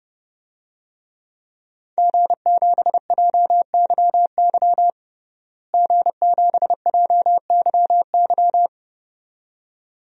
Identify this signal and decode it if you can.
Morse code, heard as a keyed tone, about 32 wpm, 700 Hz. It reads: G7JYY G7JYY
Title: DXCC-100 Call Signs - 5 Characters - Repeated 2x - 30wpm